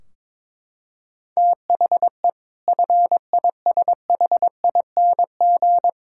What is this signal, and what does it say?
Morse code, heard as a keyed tone, about 22 wpm, 700 Hz. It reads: THE FISHING
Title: Sentences from Top 1600 Words 22wpm